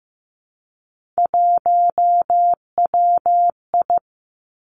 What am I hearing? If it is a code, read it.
Morse code, heard as a keyed tone, about 15 wpm, 700 Hz. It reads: 1WI
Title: Three Challenging Letter-Number Combinations 15wpm